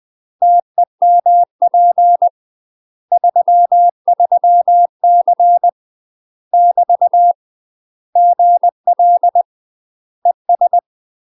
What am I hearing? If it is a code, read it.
Morse code, heard as a keyed tone, about 20 wpm, 700 Hz. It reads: TEMP 33C <BT> GL ES
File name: UK MASTER QSOs TRAINING / 20WPM / UK MASTER QSOs 0061